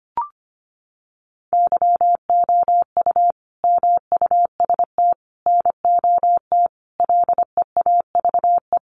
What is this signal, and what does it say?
Morse code, heard as a keyed tone, about 25 wpm, 700 Hz. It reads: YOU MUST NOT LEAVE
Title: Sentences from Top 600 Words 25wpm